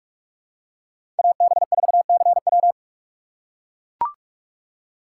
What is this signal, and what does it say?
Morse code, heard as a keyed tone, about 43 wpm, 700 Hz. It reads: AB4CW
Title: USA Call Signs - 5 Characters - 4x Character Spacing - 45wpm